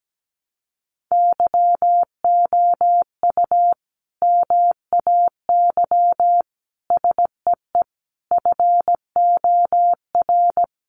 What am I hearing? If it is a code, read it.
Morse code, heard as a keyed tone, about 17 wpm, 700 Hz. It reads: YOU MAY SEE FOR
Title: Sentences from Top 600 Words 17wpm